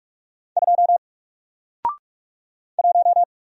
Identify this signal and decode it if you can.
Morse code, heard as a keyed tone, about 44 wpm, 700 Hz. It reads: 2 1
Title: Single Letter-Number Alt 45wpm